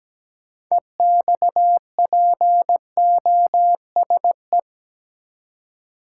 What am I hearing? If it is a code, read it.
Morse code, heard as a keyed tone, about 17 wpm, 700 Hz. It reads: EXPOSE
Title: Six Letter Words - 17wpm